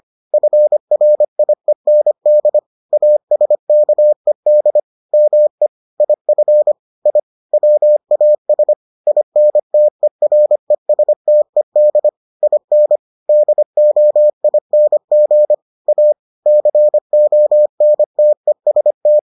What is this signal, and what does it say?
Morse code, heard as a keyed tone, about 25 wpm, 600 Hz. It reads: FRIEND ASKED ME IF I WAS INTERESTED IN DOING A CONTEST